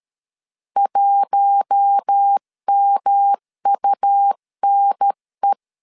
Morse code, heard as a keyed tone, about 13 wpm, 800 Hz. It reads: 1MUNE
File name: CW / DET / 1 6 1500C S C 12wpm 3533